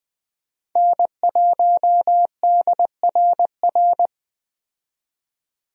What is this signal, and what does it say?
Morse code, heard as a keyed tone, about 20 wpm, 700 Hz. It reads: N1DRR
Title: USA Call Signs - 5 Characters - 20wpm